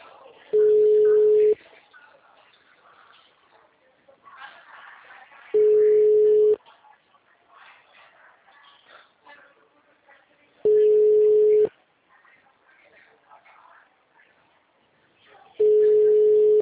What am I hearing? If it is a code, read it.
Ringback tone (Europe, 425 Hz), with speech.